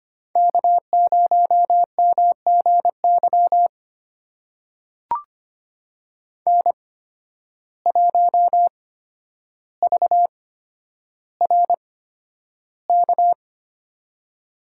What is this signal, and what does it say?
Morse code, heard as a keyed tone, about 25 wpm, spaced as for 11 wpm, 700 Hz. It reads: K0MGY N1VRK
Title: USA Call Signs - 5 Characters - 8x Character Spacing - 25wpm